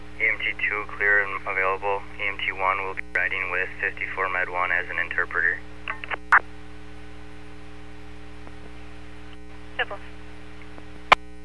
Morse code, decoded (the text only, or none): none